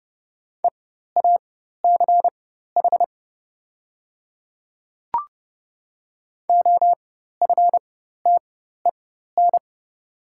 Morse code, heard as a keyed tone, about 30 wpm, spaced as for 14 wpm, 690 Hz. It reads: EACH OFTEN
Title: 4x Character Spacing - Top 300 Words - 30wpm